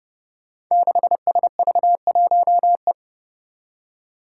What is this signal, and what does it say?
Morse code, heard as a keyed tone, about 31 wpm, 700 Hz. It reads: 6SV1E
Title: Five Challenging Letter-Number Combinations 30wpm